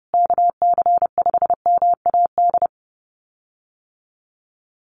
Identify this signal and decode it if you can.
Morse code, heard as a keyed tone, about 28 wpm, 700 Hz. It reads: KC5MAD
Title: USA Call Signs Repeated 3x 30wpm